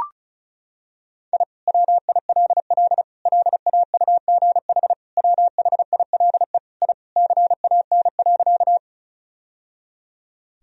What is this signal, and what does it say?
Morse code, heard as a keyed tone, about 34 wpm, 700 Hz. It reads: I WILL LAUGH WHILE I CAN.
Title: Sentences from Top 1000 Words - Review Part 1 - 35wpm